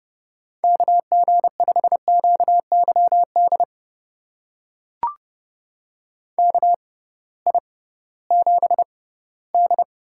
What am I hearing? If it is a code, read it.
Morse code, heard as a keyed tone, about 31 wpm, spaced as for 18 wpm, 700 Hz. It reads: KG5QYD KI7D
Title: USA Call Signs - 6 Characters - 6x Character Spacing - 30wpm